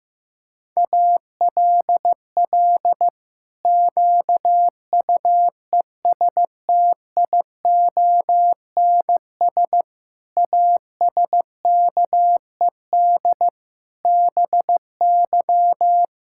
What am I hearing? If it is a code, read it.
Morse code, heard as a keyed tone, about 15 wpm, 700 Hz. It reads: ALL QUESTIONS ASKED BY
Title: Pangram Challenge v2 - 15wpm